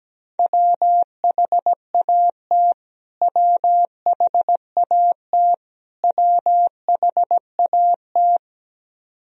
Morse code, heard as a keyed tone, about 17 wpm, 700 Hz. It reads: WHAT WHAT WHAT
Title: Top 100 Words Repeated 17wpm